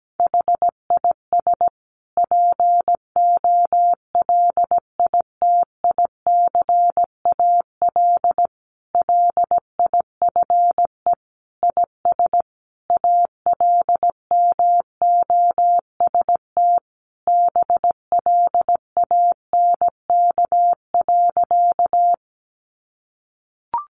Morse code, heard as a keyed tone, about 17 wpm, 700 Hz. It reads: HIS POLITICAL LIFE IS ALMOST BLANK.
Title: Sentences from Top 1500 Words 17wpm